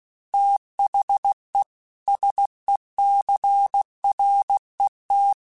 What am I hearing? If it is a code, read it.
Morse code, heard as a keyed tone, about 16 wpm, 800 Hz. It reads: THE SECRET
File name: Technical Subjects / Technical Subjects-mp3s / Sw Pwr Sup @15